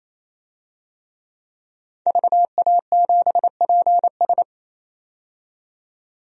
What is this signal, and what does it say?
Morse code, heard as a keyed tone, about 27 wpm, 700 Hz. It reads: VA7PS
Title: DXCC-100 Call Signs - 5 Characters - 28wpm